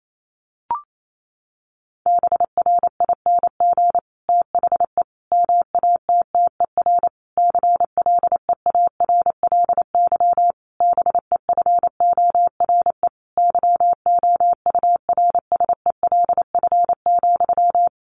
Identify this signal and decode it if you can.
Morse code, heard as a keyed tone, about 28 wpm, 700 Hz. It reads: BRING THE MATTER CLEARLY BEFORE YOURSELF,
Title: Sentences from Top 1300 Words 28wpm